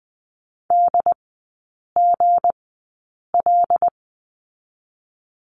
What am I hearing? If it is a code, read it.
Morse code, heard as a keyed tone, about 20 wpm, spaced as for 8 wpm, 690 Hz. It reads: DGL
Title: 159 Lesson 26e - Letter g - Three Characters 20wpm